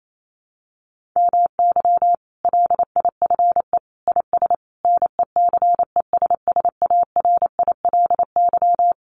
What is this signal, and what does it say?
Morse code, heard as a keyed tone, about 28 wpm, 700 Hz. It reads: MY LIFE IS NECESSARILY
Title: Sentences from Top 1400 Words 28wpm